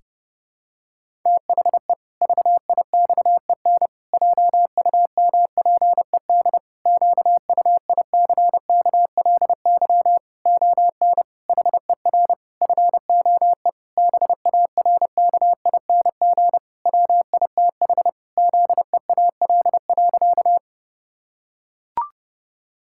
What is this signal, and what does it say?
Morse code, heard as a keyed tone, about 32 wpm, 700 Hz. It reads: THE VIXEN JUMPED QUICKLY ON HER FOE BARKING WITH ZEAL.
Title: Pangram Challenge v2 - 30wpm